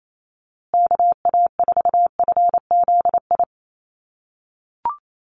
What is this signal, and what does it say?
Morse code, heard as a keyed tone, about 28 wpm, 700 Hz. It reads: KA4FZI
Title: USA Call Signs Repeated 2x 28wpm